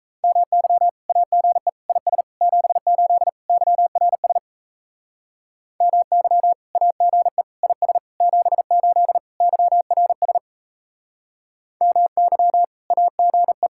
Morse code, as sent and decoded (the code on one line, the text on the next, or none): -- -.-- / .- --. / .. ... / --... ---.. / -.-- .-. ... / -- -.-- / .- --. / .. ... / --... ---.. / -.-- .-. ... / -- -.-- / .- --. .
MY AGE IS 78 YRS MY AGE IS 78 YRS MY AGE